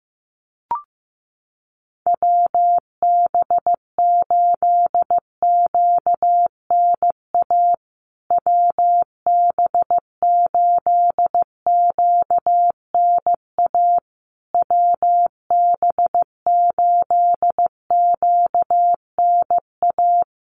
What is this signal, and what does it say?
Morse code, heard as a keyed tone, about 15 wpm, 700 Hz. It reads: WB8QNA WB8QNA WB8QNA
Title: USA Call Signs Repeated 3x 15wpm